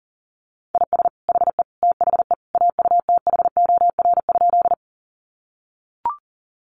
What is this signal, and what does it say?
Morse code, heard as a keyed tone, about 40 wpm, 700 Hz. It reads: IS HE THE AUTHOR?
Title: Sentences from Top 1200 Words 40wpm